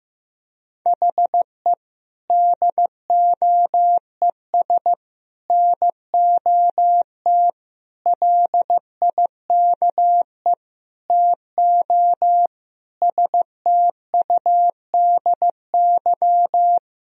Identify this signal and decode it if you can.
Morse code, heard as a keyed tone, about 15 wpm, 700 Hz. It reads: HE DOES NOT LIKE TO STUDY
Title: Sentences from Top 1000 Words - Review Part 3 - 15wpm